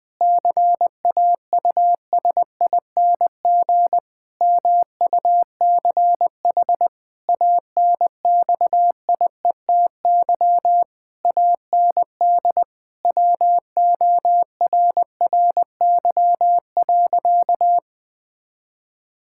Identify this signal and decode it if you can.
Morse code, heard as a keyed tone, about 20 wpm, 700 Hz. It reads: CAUSING MUCH ANXIETY AND WORRY.
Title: Pangram Challenge v2 - 20wpm